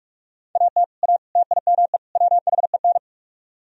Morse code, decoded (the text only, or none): AT A TIME WHEN